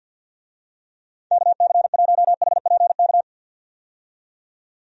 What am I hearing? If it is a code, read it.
Morse code, heard as a keyed tone, about 46 wpm, 700 Hz. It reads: KC1HPK
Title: USA Call Signs - 6 Characters - 50wpm